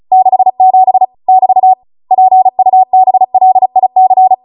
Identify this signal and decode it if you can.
Morse code, heard as a keyed tone, about 36 wpm, 750 Hz. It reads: 67 <BT> PUBLIC